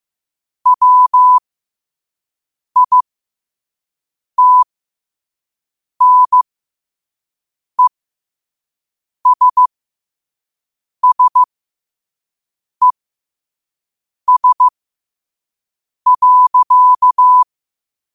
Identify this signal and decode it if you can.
Morse code, heard as a keyed tone, about 15 wpm, spaced as for 5 wpm, 1000 Hz. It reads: WITNESSES.
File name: qotdfiles / 5wpm